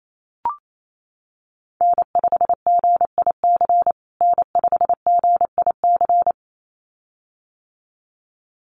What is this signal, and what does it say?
Morse code, heard as a keyed tone, about 28 wpm, 700 Hz. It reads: N5GIC N5GIC
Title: USA Call Signs Repeated 2x 28wpm